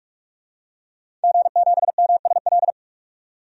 Morse code, heard as a keyed tone, about 46 wpm, 700 Hz. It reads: G7MSL
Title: DXCC-100 Call Signs - 5 Characters - 45wpm